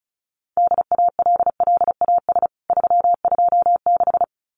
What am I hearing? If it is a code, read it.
Morse code, heard as a keyed tone, about 34 wpm, 700 Hz. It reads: DALLAS 326
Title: CWT Exchange 35wpm